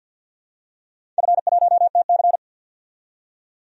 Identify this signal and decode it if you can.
Morse code, heard as a keyed tone, about 47 wpm, 700 Hz. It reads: F1TC
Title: DXCC-100 Call Signs - 4 Characters - 50wpm